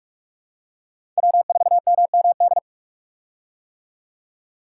Morse code, heard as a keyed tone, about 46 wpm, 700 Hz. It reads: W4MMD
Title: K1USN SST QSO Practice - 45wpm